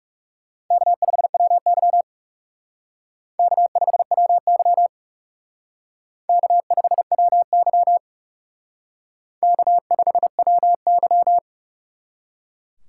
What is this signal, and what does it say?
Morse code, heard as a keyed tone, about 36 wpm, 700 Hz. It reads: K5WY K5WY K5WY K5WY
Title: Speed Racing - USA 1x2 Call Signs 30wpm